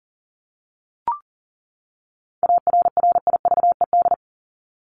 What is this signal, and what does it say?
Morse code, heard as a keyed tone, about 37 wpm, 700 Hz. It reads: ARRIVED
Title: ED Suffix - Top 100 Words with Base Words 40wpm